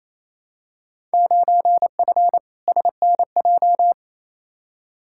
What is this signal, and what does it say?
Morse code, heard as a keyed tone, about 28 wpm, 700 Hz. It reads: 9F SNJ
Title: Field Day Exchanges Encore - 28wpm